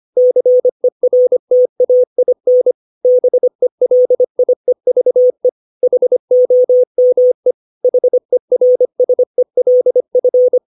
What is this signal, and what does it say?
Morse code, heard as a keyed tone, about 25 wpm, 500 Hz. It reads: CERTAIN BELIEVE HOME HERSELF
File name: 632 words / 632 Words 25WPM